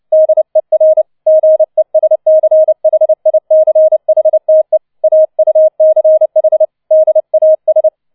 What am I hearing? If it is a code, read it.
Morse code, heard as a keyed tone, about 29 wpm, 630 Hz. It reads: DER GESCHICHTE AUCH DAS